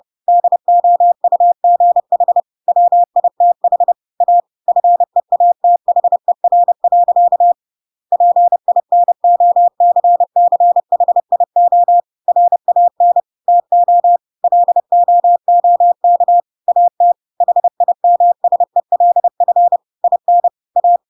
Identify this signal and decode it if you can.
Morse code, heard as a keyed tone, about 30 wpm, 700 Hz. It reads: DOUGH WITH A FEATHER. PINOCCHIO RAN TO LOOK AT HIMSELF IN A
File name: Pinocchio / Pinocchio ch8 30 30 ik1hge